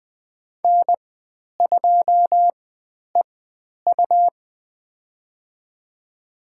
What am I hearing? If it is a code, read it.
Morse code, heard as a keyed tone, about 20 wpm, spaced as for 10 wpm, 700 Hz. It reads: N2EU
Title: 109 Lesson 18g - Letter m - Call Signs 20wpm